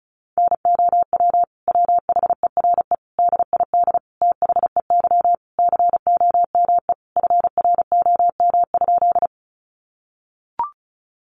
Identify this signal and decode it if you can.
Morse code, heard as a keyed tone, about 34 wpm, 700 Hz. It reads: NOW WHERE DID THEY COME FROM?
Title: Sentences from Top 200 Words 35wpm